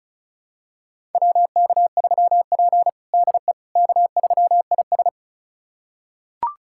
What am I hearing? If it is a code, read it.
Morse code, heard as a keyed tone, about 36 wpm, 700 Hz. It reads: WK3P DE K3IS
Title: USA Callsign DE Callsign - 4 Characters - 35wpm